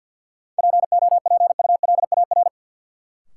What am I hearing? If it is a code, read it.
Morse code, heard as a keyed tone, about 53 wpm, 700 Hz. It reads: POPULAR
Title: Sentences from Top 1000 Words - Review Part 3 - New Word Drill - 50wpm